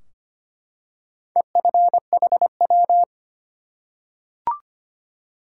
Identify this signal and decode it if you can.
Morse code, heard as a keyed tone, about 25 wpm, 700 Hz. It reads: EFHW
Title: Top 180 QSO Elements 25wpm